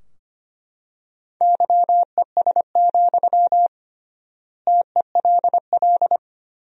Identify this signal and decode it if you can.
Morse code, heard as a keyed tone, about 25 wpm, 700 Hz. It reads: YES, TELL
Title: 3x Word-Spacing - Sentences from Top 500 Words - 25wpm